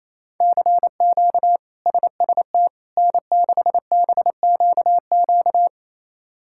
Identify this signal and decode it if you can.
Morse code, heard as a keyed tone, about 28 wpm, 700 Hz. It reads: CQ SST N6BQQ
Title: K1USN SST QSO Practice - 28wpm